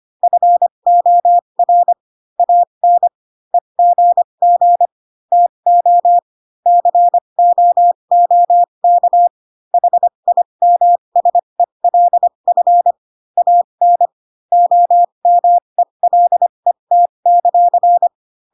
Morse code, heard as a keyed tone, about 25 wpm, spaced as for 22 wpm, 700 Hz. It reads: FOR AN EGG TO COOK HIMSELF AN OMELET;